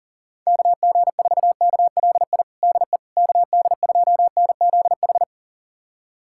Morse code, heard as a keyed tone, about 40 wpm, 700 Hz. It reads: KG4KLI DE KD2NZH
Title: USA Callsign DE Callsign 40wpm